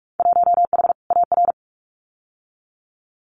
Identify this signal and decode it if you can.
Morse code, heard as a keyed tone, about 46 wpm, 700 Hz. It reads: 1H AR
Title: Winter Field Day Exchanges 45wpm